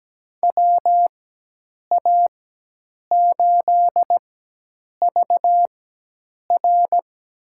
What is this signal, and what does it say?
Morse code, heard as a keyed tone, about 17 wpm, spaced as for 8 wpm, 700 Hz. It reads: WA8VR
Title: USA Call Signs - 4x Character Spacing - 17wpm